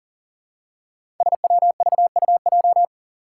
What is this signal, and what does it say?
Morse code, heard as a keyed tone, about 40 wpm, 700 Hz. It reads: SWVUJ
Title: Five Challenging Letter-Number Combinations 40wpm